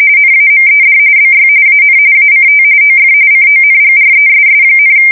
RTTY, about 45.45 baud. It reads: NAMED BY AN ACT OF CONGRESS,
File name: fsk170 / battleships